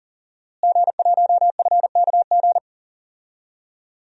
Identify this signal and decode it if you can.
Morse code, heard as a keyed tone, about 40 wpm, 700 Hz. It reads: G1FKG
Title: DXCC-100 Call Signs - 5 Characters - Repeated 2x - 40wpm